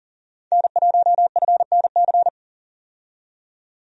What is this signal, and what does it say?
Morse code, heard as a keyed tone, about 40 wpm, 700 Hz. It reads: N1FNC
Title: USA Call Signs - 5 Characters - 40wpm